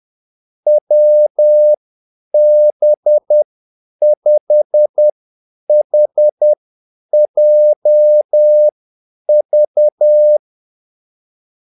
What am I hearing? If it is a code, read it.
Morse code, heard as a keyed tone, about 10 wpm, spaced as for 8 wpm, 600 Hz. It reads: WB5HJV
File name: Callsigns-10-8wpm